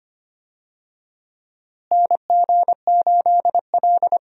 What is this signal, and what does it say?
Morse code, heard as a keyed tone, about 25 wpm, 700 Hz. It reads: NG8L
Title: USA Call Signs - 4 Characters - Repeated 3x - 25wpm